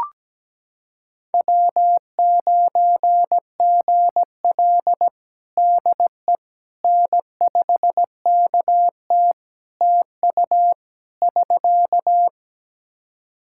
Morse code, heard as a keyed tone, about 17 wpm, 700 Hz. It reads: W9GL DE N5KT TU <SK>
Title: Practice QSO - Exchange 4 - Ending - 17wpm